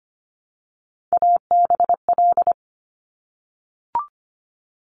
Morse code, heard as a keyed tone, about 25 wpm, 700 Hz. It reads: ABL
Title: Top Three Letter Combinations in Words 25wpm